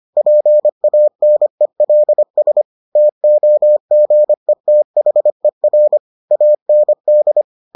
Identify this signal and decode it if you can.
Morse code, heard as a keyed tone, about 25 wpm, 600 Hz. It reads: PANELS TOGETHER AND